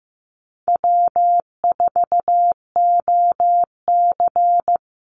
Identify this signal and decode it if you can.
Morse code, heard as a keyed tone, about 15 wpm, 700 Hz. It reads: W4OC
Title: Easy Call Signs Repeated 2x - 15wpm